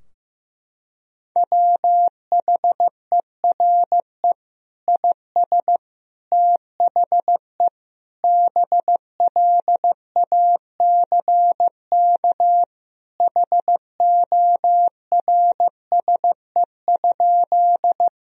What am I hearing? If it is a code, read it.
Morse code, heard as a keyed tone, about 16 wpm, 700 Hz. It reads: WHERE IS THE BLACK HORSE?
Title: Sentences from Top 1000 Words - Review Part 2 - 15wpm